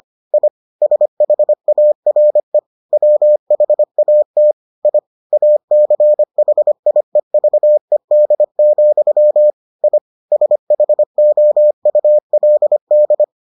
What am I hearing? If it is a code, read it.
Morse code, heard as a keyed tone, about 25 wpm, 600 Hz. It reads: I SHARE WHAT I ACHIEVED, I SHOULD